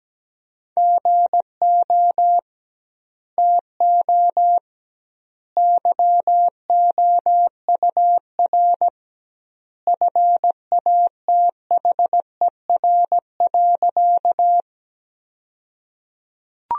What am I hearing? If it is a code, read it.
Morse code, heard as a keyed tone, about 17 wpm, 700 Hz. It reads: GO TO YOUR FATHER.
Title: 2x Word-Spacing - Sentences from Top 300 Words - 17wpm